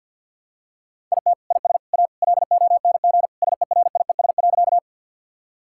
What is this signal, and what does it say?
Morse code, heard as a keyed tone, about 51 wpm, 700 Hz. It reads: IT IS A LONG SERIES.
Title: Sentences from Top 1000 Words - Review Part 5 - 50wpm